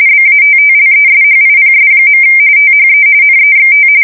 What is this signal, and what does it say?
RTTY, about 45.45 baud. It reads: BATTLESHIPS. THE NEXT
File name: fsk170 / USS-New-Jersey